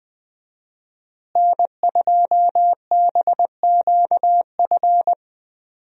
Morse code, heard as a keyed tone, about 20 wpm, 700 Hz. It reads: N2BQF
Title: USA Call Signs - 5 Characters - Repeated 3x - 20wpm